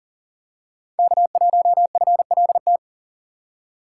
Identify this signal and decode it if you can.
Morse code, heard as a keyed tone, about 40 wpm, 700 Hz. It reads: K1FLT
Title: USA Call Signs - 5 Characters - 4x Character Spacing - 40wpm